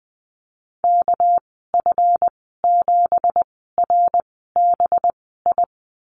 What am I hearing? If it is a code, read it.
Morse code, heard as a keyed tone, about 20 wpm, spaced as for 14 wpm, 700 Hz. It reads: KF7RBI
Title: 200 Lesson 32g - Letter k - Call Signs 20wpm